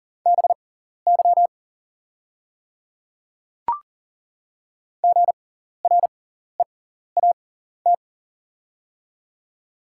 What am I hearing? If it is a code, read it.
Morse code, heard as a keyed tone, about 40 wpm, spaced as for 14 wpm, 690 Hz. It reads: BY GREAT
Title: 6x Character Spacing - Top 200 Words - 40wpm